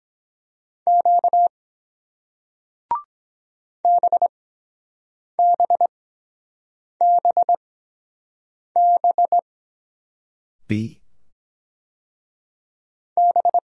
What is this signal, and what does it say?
Morse code, heard as a keyed tone, about 23 wpm, spaced as for 7 wpm, 700 Hz. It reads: Q BBBB B